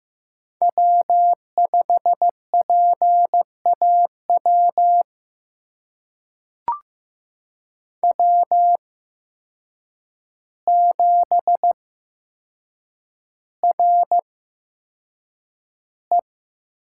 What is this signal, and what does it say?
Morse code, heard as a keyed tone, about 15 wpm, spaced as for 7 wpm, 700 Hz. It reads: W5PAW W7RE
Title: USA Call Signs - 8x Character Spacing - 15wpm